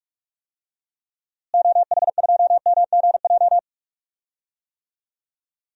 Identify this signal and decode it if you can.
Morse code, heard as a keyed tone, about 46 wpm, 700 Hz. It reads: OH2MGJ
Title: DXCC-100 Call Signs - 6 Characters - Repeated 3x - 45wpm